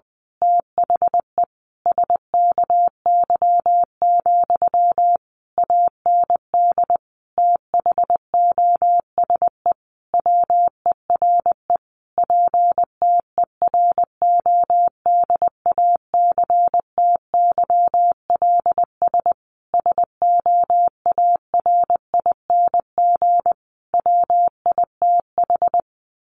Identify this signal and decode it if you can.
Morse code, heard as a keyed tone, about 20 wpm, 700 Hz. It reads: THE SKY, AND THOSE WERE PTERODACTYLS SOARING WITH
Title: A Sound Of Thunder 20wpm